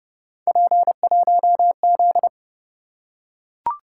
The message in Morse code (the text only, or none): P1Z